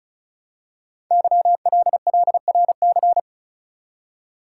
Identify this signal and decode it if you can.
Morse code, heard as a keyed tone, about 36 wpm, 700 Hz. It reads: YLLRC
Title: Code Groups - 5 Letters - 35wpm